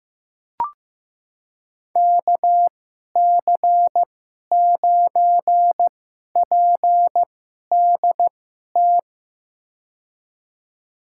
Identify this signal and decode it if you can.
Morse code, heard as a keyed tone, about 15 wpm, spaced as for 11 wpm, 700 Hz. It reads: KC9PDT
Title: USA Call Signs - 6 Characters - 2x Character Spacing - 15wpm